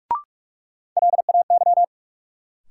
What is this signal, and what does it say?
Morse code, heard as a keyed tone, about 46 wpm, 700 Hz. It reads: LAY
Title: Sentences from Top 400 Words - New Word Drill - 45wpm